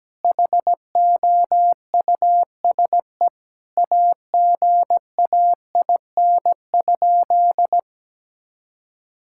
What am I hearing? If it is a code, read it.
Morse code, heard as a keyed tone, about 17 wpm, 700 Hz. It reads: HOUSE AGAIN?
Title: Sentences from Top 1000 Words - Review Part 6 - 17wpm